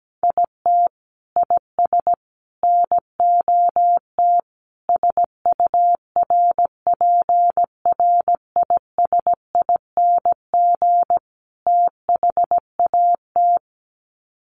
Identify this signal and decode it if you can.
Morse code, heard as a keyed tone, about 17 wpm, 700 Hz. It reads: IT IS NOT SURPRISING THAT
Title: Sets of 5 Words 17wpm